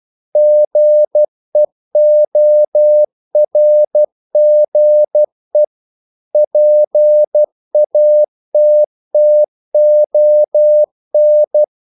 Morse code, heard as a keyed tone, about 12 wpm, 600 Hz. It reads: GEORGE PATTON
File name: CW / Practice / EnglishQuotations0312